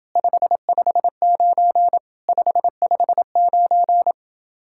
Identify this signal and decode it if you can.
Morse code, heard as a keyed tone, about 28 wpm, 700 Hz. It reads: 559 559